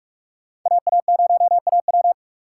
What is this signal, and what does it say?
Morse code, heard as a keyed tone, about 60 wpm, spaced as for 52 wpm, 700 Hz. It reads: AA0AW
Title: Speed Racing - USA 2x2 Call Signs 45wpm